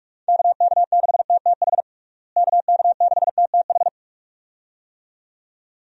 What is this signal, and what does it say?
Morse code, heard as a keyed tone, about 41 wpm, 700 Hz. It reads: KK6TTH KK6TTH
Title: USA Call Signs - 6 Characters - Repeated 2x - 45wpm